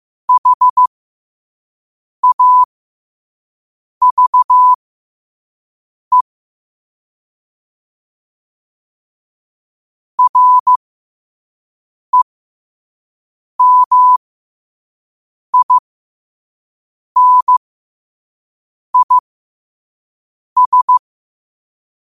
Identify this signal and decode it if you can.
Morse code, heard as a keyed tone, about 15 wpm, spaced as for 5 wpm, 1000 Hz. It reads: HAVE REMINIS
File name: qotdfiles / 5wpm